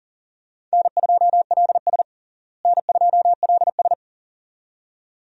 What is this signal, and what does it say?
Morse code, heard as a keyed tone, about 40 wpm, 700 Hz. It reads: N2LS N2LS
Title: USA Call Signs - 4 Characters - Repeated 2x - 40wpm